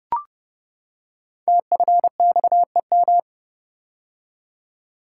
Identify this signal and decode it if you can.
Morse code, heard as a keyed tone, about 28 wpm, 700 Hz. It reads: TFXEM
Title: Code Groups - 5 Letters - 30wpm